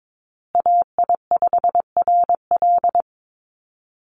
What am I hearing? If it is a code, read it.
Morse code, heard as a keyed tone, about 22 wpm, 700 Hz. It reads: AI5RL
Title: USA Call Signs Repeated 2x 22wpm